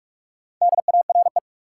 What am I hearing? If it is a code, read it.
Morse code, heard as a keyed tone, about 46 wpm, 700 Hz. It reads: DARE
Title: Four Letter Words - 45wpm